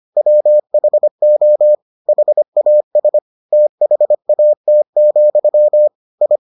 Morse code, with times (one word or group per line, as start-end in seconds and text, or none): 0.2-1.7 WHO
2.1-3.2 HAS
3.5-5.9 THAT,
6.2-6.4 I